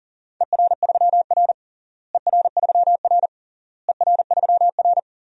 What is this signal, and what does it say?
Morse code, heard as a keyed tone, about 40 wpm, 700 Hz. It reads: ER3R ER3R ER3R
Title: DXCC-100 Call Signs - 4 Characters - Repeated 3x - 40wpm